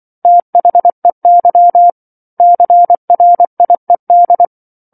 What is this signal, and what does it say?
Morse code, heard as a keyed tone, about 24 wpm, 700 Hz. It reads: THEY CRIED
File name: 07-Chapter-07 24 WPM